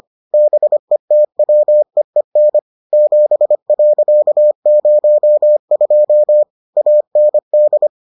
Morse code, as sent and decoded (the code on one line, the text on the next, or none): -... - .-- . -. / --... .-.-.- ----- ..--- / .- -. -..
BETWEEN 7.02 AND